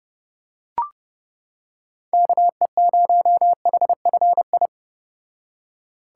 Morse code, heard as a keyed tone, about 31 wpm, 700 Hz. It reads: KE0HFI